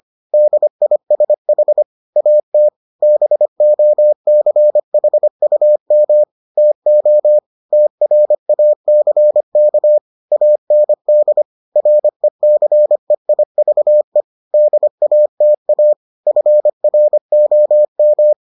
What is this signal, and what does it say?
Morse code, heard as a keyed tone, about 25 wpm, 600 Hz. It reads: DISH AT BOCHUM TO TRACK AND RECEIVE DATA FROM